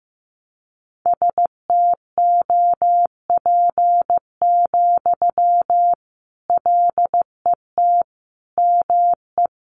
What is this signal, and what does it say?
Morse code, heard as a keyed tone, about 15 wpm, 700 Hz. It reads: STOP, LET ME
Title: Sentences from Top 300 Words 15wpm